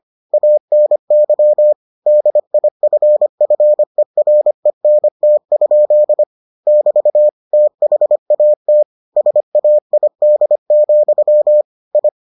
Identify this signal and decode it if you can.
Morse code, heard as a keyed tone, about 25 wpm, 600 Hz. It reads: ANY DIFFERENT? <BT> THAT SAID, I